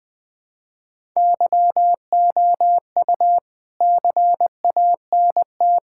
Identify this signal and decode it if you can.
Morse code, heard as a keyed tone, about 21 wpm, 700 Hz. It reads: YOU CANT
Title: Sets of 9 Words - 20wpm